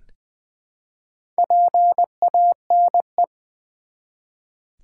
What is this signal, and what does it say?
Morse code, heard as a keyed tone, about 20 wpm, 700 Hz. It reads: PANE
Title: Build A Word - 1 Letter At A Time - 5-Letter Words - 20wpm